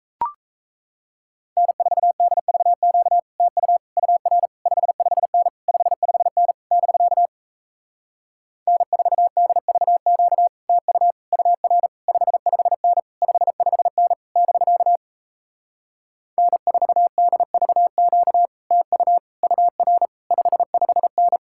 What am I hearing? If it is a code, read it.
Morse code, heard as a keyed tone, about 38 wpm, 700 Hz. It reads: N4DVQ TU UR 55N 55N <BK> N4DVQ TU UR 55N 55N <BK> N4DVQ TU UR 55N